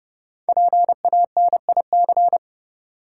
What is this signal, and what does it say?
Morse code, heard as a keyed tone, about 30 wpm, 700 Hz. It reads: PANIC
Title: Five Letter Words - 30wpm